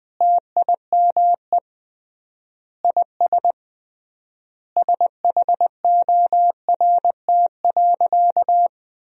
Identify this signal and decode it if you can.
Morse code, heard as a keyed tone, about 20 wpm, 700 Hz. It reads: TIME IS SHORT.
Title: 3x Word-Spacing - Sentences from Top 500 Words - 20wpm